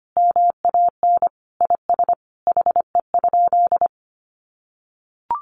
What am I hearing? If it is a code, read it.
Morse code, heard as a keyed tone, about 25 wpm, 700 Hz. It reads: MAN IS HE?
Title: Sentences from Top 1300 Words 25wpm